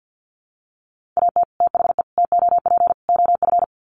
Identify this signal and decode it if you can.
Morse code, heard as a keyed tone, about 50 wpm, 700 Hz. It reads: AT THE TOP OF
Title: Sets of 4 Words 50wpm